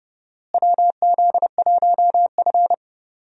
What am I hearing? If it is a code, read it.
Morse code, heard as a keyed tone, about 32 wpm, 700 Hz. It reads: WZ1F
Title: USA Call Signs - 4 Characters - Repeated 3x - 30wpm